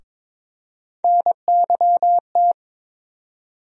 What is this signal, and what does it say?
Morse code, heard as a keyed tone, about 22 wpm, 700 Hz. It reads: NYT
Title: USA Call Sign Postfixes - 3 Letters 22wpm